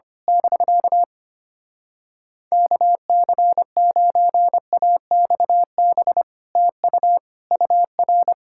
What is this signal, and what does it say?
Morse code, heard as a keyed tone, about 24 wpm, 700 Hz. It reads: <BK> KC9AXB TU UR
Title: Speed-Racing Common QSO Phrases - 25wpm